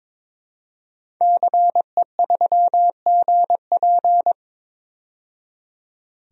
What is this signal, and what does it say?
Morse code, heard as a keyed tone, about 22 wpm, 700 Hz. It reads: CE3GP
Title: DXCC-100 Call Signs - 5 Characters - 22wpm